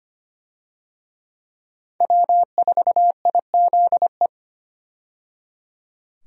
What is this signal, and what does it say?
Morse code, heard as a keyed tone, about 25 wpm, 700 Hz. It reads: W4IZE
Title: USA Call Signs - 5 Characters - 25wpm